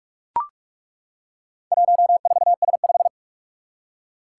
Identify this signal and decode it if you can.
Morse code, heard as a keyed tone, about 43 wpm, 700 Hz. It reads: 14S5
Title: Four Challenging Letter-Number Combinations 45wpm